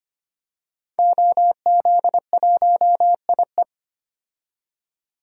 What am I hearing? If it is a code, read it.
Morse code, heard as a keyed tone, about 25 wpm, 700 Hz. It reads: OZ1IE
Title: DXCC-100 Call Signs - 5 Characters - 25wpm